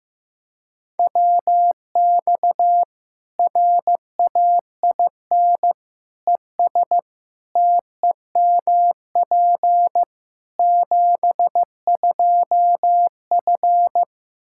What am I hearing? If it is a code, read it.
Morse code, heard as a keyed tone, about 15 wpm, 700 Hz. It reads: WX RAIN ES TEMP 72F
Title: K1USN SST QSO Practice 15wpm